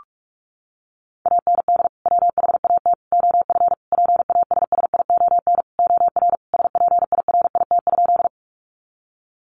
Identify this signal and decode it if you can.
Morse code, heard as a keyed tone, about 44 wpm, 700 Hz. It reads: AND WHAT OF PASSION OR SPIRIT?
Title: Sentences from Top 1400 Words 45wpm